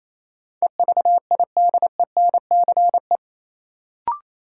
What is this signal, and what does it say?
Morse code, heard as a keyed tone, about 28 wpm, 700 Hz. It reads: EVIDENCE
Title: Sets of 5 Words Encore - New Word Drill - 28wpm